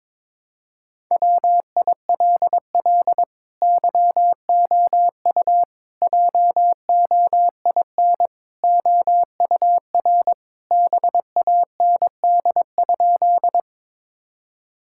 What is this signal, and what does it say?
Morse code, heard as a keyed tone, about 22 wpm, 700 Hz. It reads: WILL YOU JOIN OUR BAND?
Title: Sentences from Top 1600 Words 22wpm